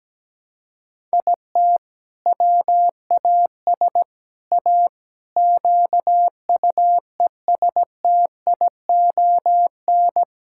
Morse code, heard as a keyed tone, about 17 wpm, 700 Hz. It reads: IT WAS A QUESTION
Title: Sentences from Top 1000 Words - Review Part 6 - 17wpm